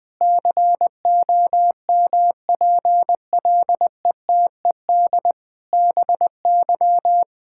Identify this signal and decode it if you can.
Morse code, heard as a keyed tone, about 20 wpm, 700 Hz. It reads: COMPLETED BY
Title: Sentences with 8-Word Phrases - Spoken Beginning - 20wpm